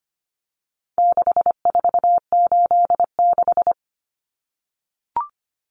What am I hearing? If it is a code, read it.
Morse code, heard as a keyed tone, about 25 wpm, 700 Hz. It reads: 6486
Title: Numbers - 4-Digit 25wpm